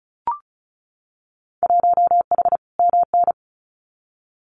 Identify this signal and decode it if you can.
Morse code, heard as a keyed tone, about 34 wpm, 700 Hz. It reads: 1H MN
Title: Winter Field Day Exchanges 35wpm